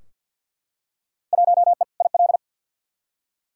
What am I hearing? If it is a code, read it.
Morse code, heard as a keyed tone, about 53 wpm, 700 Hz. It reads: 1E IL
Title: Field Day Exchanges - 50wpm